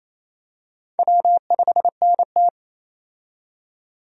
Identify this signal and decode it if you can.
Morse code, heard as a keyed tone, about 28 wpm, 700 Hz. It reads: W5NT
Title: USA Call Signs - 4 Characters - 28wpm